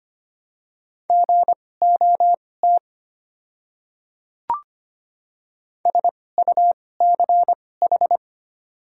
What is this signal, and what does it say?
Morse code, heard as a keyed tone, about 25 wpm, spaced as for 18 wpm, 700 Hz. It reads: GOT SUCH
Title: 2x Character Spacing - Top 500 Words - 25wpm